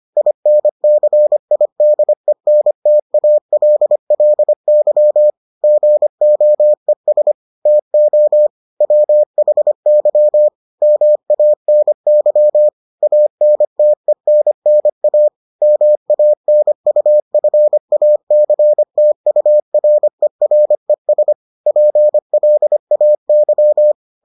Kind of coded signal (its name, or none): Morse code